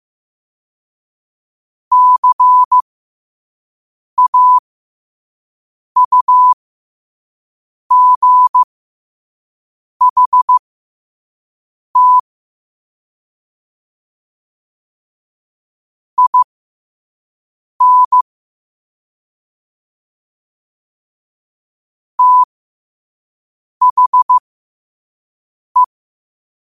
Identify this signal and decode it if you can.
Morse code, heard as a keyed tone, about 15 wpm, spaced as for 5 wpm, 1000 Hz. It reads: CAUGHT IN THE
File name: qotdfiles / 5wpm